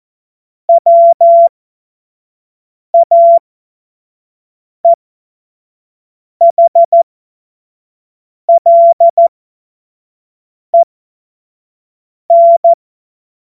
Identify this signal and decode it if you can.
Morse code, heard as a keyed tone, about 14 wpm, spaced as for 5 wpm, 680 Hz. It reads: WAEHLEN
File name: Morsekurs / Witze / Witz 21. Dezember 2021, 50 F70